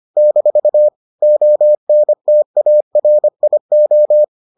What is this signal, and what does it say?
Morse code, heard as a keyed tone, about 25 wpm, 600 Hz. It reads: - ONTARIO